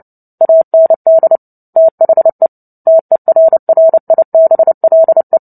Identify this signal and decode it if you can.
Morse code, heard as a keyed tone, about 30 wpm, 650 Hz. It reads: AND THE TERRIBLE
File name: Book Two / 03-Chapter 3 - 29 WPM